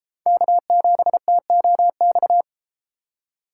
Morse code, heard as a keyed tone, about 32 wpm, 700 Hz. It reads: K7TOX